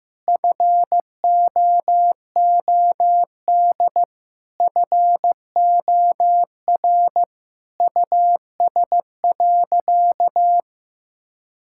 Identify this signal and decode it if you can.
Morse code, heard as a keyed tone, about 15 wpm, 700 Hz. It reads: FOOD FOR US.
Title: Sentences from Top 1000 Words - Review Part 5 - 15wpm